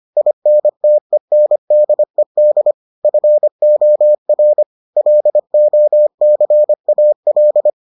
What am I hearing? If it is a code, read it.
Morse code, heard as a keyed tone, about 25 wpm, 600 Hz. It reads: INTENDED FOR LOCAL